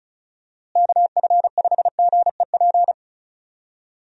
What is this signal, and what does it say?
Morse code, heard as a keyed tone, about 36 wpm, 700 Hz. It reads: KF5GEP
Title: USA Call Signs - 6 Characters - Repeated 2x - 35wpm